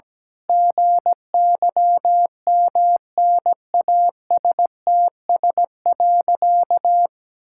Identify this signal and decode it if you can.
Morse code, heard as a keyed tone, about 17 wpm, 700 Hz. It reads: GYMNASTS.
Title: Pangram Challenge v2 - 17wpm